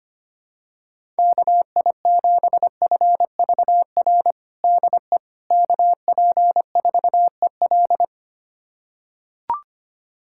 Morse code, heard as a keyed tone, about 25 wpm, 700 Hz. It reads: KI7FVR DE KP4EL
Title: USA Callsign DE Callsign 25wpm